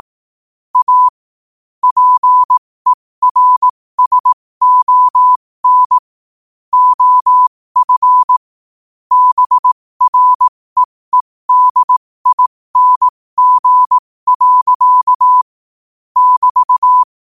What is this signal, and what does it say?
Morse code, heard as a keyed tone, about 18 wpm, spaced as for 15 wpm, 1000 Hz. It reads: A PERSON OF BREEDING. <BT>
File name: qotdfiles / 15wpm